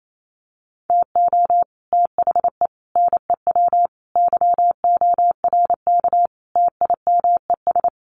Morse code, heard as a keyed tone, about 28 wpm, 700 Hz. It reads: TO THE NEW YORK TIMES
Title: Sets of 5 Words Encore2 28wpm